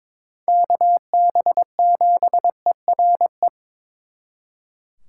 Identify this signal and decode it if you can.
Morse code, heard as a keyed tone, about 26 wpm, 700 Hz. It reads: KB7ERE
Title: Speed Racing - USA 2x3 Call Signs 22wpm